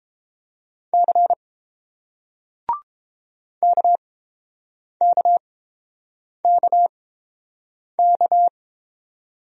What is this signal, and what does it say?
Morse code, heard as a keyed tone, about 31 wpm, spaced as for 7 wpm, 700 Hz. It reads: C KKKK